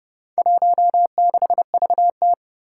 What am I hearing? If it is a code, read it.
Morse code, heard as a keyed tone, about 28 wpm, 700 Hz. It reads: 16VT
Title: Four Challenging Letter-Number Combinations 30wpm